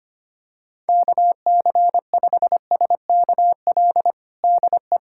Morse code, heard as a keyed tone, about 25 wpm, 700 Hz. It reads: KC5SKL DE
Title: USA Callsign DE Callsign - 6 Characters - 25wpm